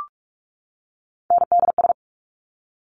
Morse code, heard as a keyed tone, about 46 wpm, 690 Hz. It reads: NDS
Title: Top Three Letter Combinations in Words 45wpm